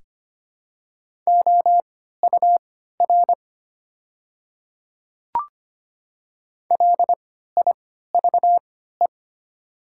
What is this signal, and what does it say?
Morse code, heard as a keyed tone, about 25 wpm, spaced as for 14 wpm, 700 Hz. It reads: OUR LIVE
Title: 3x Character Spacing - Top 200 Words - 25wpm